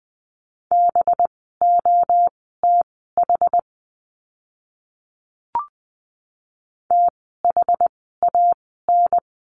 Morse code, heard as a keyed tone, about 20 wpm, spaced as for 14 wpm, 700 Hz. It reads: BOTH THAN
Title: 185 Lesson 30f - Letter b - Words 20wpm